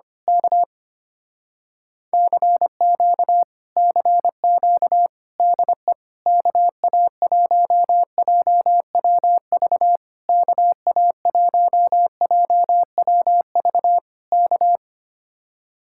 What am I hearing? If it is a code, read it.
Morse code, heard as a keyed tone, about 25 wpm, 700 Hz. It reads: K CQ CQ DE KA1JWV KA1JWV K
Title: Speed-Racing Common QSO Phrases - 25wpm